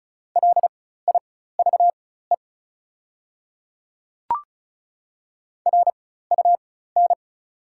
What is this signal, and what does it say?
Morse code, heard as a keyed tone, about 34 wpm, spaced as for 16 wpm, 700 Hz. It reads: LIVE RUN
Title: 4x Character Spacing - Top 300 Words - 35wpm